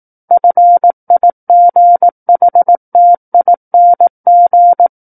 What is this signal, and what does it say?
Morse code, heard as a keyed tone, about 18 wpm, 700 Hz. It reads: FIGHTING
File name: 18 WPM